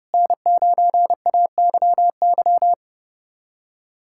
Morse code, heard as a keyed tone, about 30 wpm, 700 Hz. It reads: N9AYY
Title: USA Callsign DE Callsign - 5 Characters - 30wpm